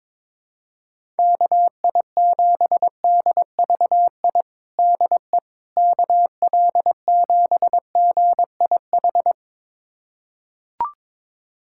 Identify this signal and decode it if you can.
Morse code, heard as a keyed tone, about 22 wpm, 700 Hz. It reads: KI7DVI DE KL7GIH
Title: USA Callsign DE Callsign - 6 Characters - 22wpm